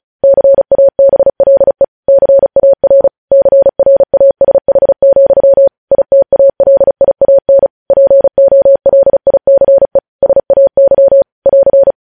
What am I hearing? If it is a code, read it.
Morse code, heard as a keyed tone, about 36 wpm, 550 Hz. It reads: CABLE CAR CRASH, ITALIAN POLICE SAY <AR>
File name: podcast / News headlines-1x-35x35